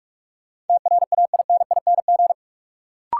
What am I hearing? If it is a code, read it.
Morse code, heard as a keyed tone, about 46 wpm, 700 Hz. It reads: TRAINING
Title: Sentences from Top 1000 Words - Review Part 4 - New Word Drill - 45wpm